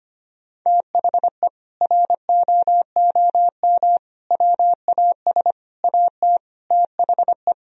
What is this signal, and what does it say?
Morse code, heard as a keyed tone, about 25 wpm, 700 Hz. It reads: THE ROOM WAS AT THE
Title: Sentences from Top 1000 Words - Review Part 2 - 25wpm